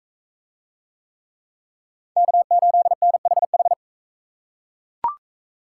Morse code, heard as a keyed tone, about 40 wpm, 700 Hz. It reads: K8NHH